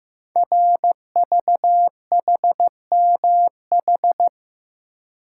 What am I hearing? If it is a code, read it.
Morse code, heard as a keyed tone, about 15 wpm, 700 Hz. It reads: RVHMH
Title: Code Groups - 5 Letters - 15wpm